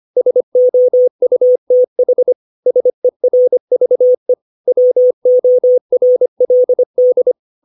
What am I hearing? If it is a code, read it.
Morse code, heard as a keyed tone, about 25 wpm, 500 Hz. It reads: SOUTH SERVE WORLD
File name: 632 words / 632 Words 25WPM